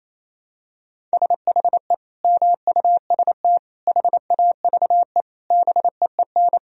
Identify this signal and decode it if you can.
Morse code, heard as a keyed tone, about 28 wpm, 700 Hz. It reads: SHE MUST HAVE BEEN
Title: Sentences from Top 1000 Words - Review Part 2 - 28wpm